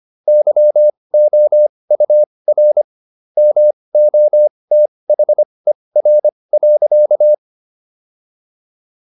Morse code, heard as a keyed tone, about 25 wpm, spaced as for 20 wpm, 600 Hz. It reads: YOUR MOTHER.
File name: SS123 20